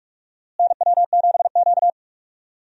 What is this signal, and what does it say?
Morse code, heard as a keyed tone, about 45 wpm, 700 Hz. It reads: NW7Q
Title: USA Call Signs - 4 Characters - Repeated 3x - 45wpm